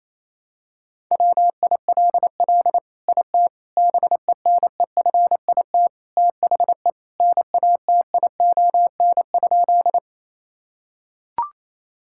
Morse code, heard as a keyed tone, about 28 wpm, 690 Hz. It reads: WILL IT BENEFIT THE NATION?
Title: Sentences from Top 1000 Words - Review Part 6 - 28wpm